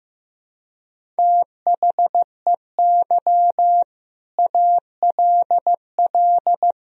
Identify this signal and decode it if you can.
Morse code, heard as a keyed tone, about 15 wpm, 700 Hz. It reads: THEY ALL
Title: Sentences from Top 1000 Words - Review Part 4 - 15wpm